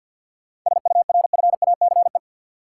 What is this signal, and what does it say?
Morse code, heard as a keyed tone, about 53 wpm, 700 Hz. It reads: SURFACE
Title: Sentences from Top 600 Words - New Word Drill - 50wpm